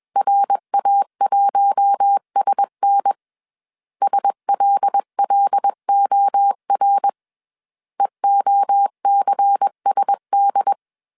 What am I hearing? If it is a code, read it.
Morse code, heard as a keyed tone, about 21 wpm, 800 Hz. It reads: RA1SN SLLOR EOCSD
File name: CW / DET / 1 6 1500C S A 20wpm 2048